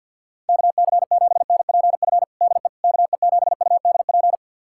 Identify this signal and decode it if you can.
Morse code, heard as a keyed tone, about 52 wpm, 700 Hz. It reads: KC7NPF DE KE7UDP
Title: USA Callsign DE Callsign - 6 Characters - 50wpm